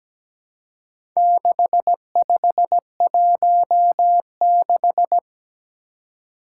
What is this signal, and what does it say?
Morse code, heard as a keyed tone, about 17 wpm, 700 Hz. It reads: 6516
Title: Four Challenging Letter-Number Combinations 17wpm